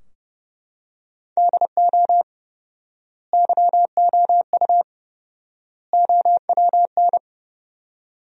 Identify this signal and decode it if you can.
Morse code, heard as a keyed tone, about 30 wpm, 700 Hz. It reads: DO YOU OWN
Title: 4x Word-Spacing - Sentences from Top 400 Words - 30wpm